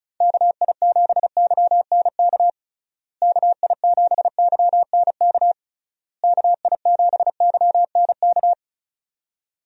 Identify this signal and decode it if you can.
Morse code, heard as a keyed tone, about 36 wpm, 700 Hz. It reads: KI7YNK KI7YNK KI7YNK
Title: USA Call Signs - 6 Characters - Repeated 3x - 35wpm